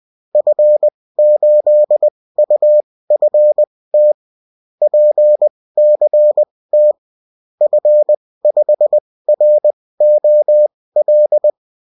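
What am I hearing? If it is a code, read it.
Morse code, heard as a keyed tone, about 20 wpm, spaced as for 16 wpm, 600 Hz. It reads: F8UFT PCT F5ROL
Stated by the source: Bulletin in text and CW form.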